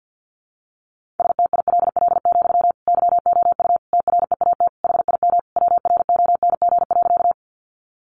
Morse code, heard as a keyed tone, about 50 wpm, 700 Hz. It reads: STILL, YOU TREAT HIM WRONG.
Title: Sentences from Top 1300 Words 50wpm